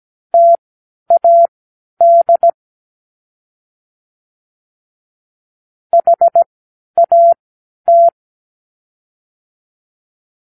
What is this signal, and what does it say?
Morse code, heard as a keyed tone, about 17 wpm, spaced as for 10 wpm, 680 Hz. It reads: TAD HAT
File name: k7qo / Practicewords